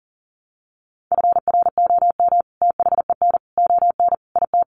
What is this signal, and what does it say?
Morse code, heard as a keyed tone, about 40 wpm, 700 Hz. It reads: FROM THEN ON IT
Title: Sentences from Top 400 Words 40wpm